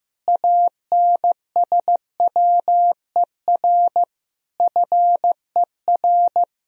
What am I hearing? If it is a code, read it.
Morse code, heard as a keyed tone, about 14 wpm, 700 Hz. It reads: ANSWER FER
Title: Sentences from Top 300 Words and Abbreviations 15wpm